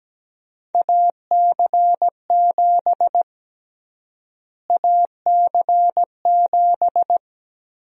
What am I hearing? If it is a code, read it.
Morse code, heard as a keyed tone, about 17 wpm, 700 Hz. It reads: AC7 AC7
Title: USA Call Sign Prefixes - Repeated 2x - 17wpm